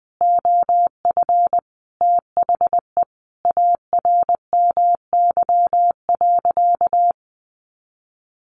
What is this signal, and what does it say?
Morse code, heard as a keyed tone, about 20 wpm, 700 Hz. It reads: OF THE ARMY.
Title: Sentences from Top 1400 Words 20wpm